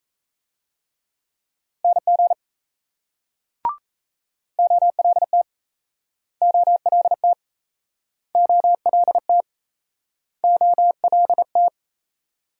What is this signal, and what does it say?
Morse code, heard as a keyed tone, about 38 wpm, 700 Hz. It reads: NG OLT OLT OLT OLT